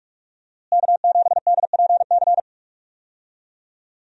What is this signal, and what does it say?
Morse code, heard as a keyed tone, about 41 wpm, 700 Hz. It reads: K7DPC
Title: USA Call Signs - 5 Characters - Repeated 2x - 45wpm